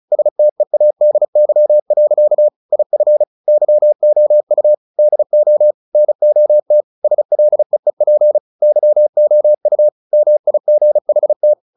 Morse code, heard as a keyed tone, about 34 wpm, 600 Hz. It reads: STEADY. IF YOU DO NOT SLEEP YOU MIGHT
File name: the old man and the sea 2 35wpm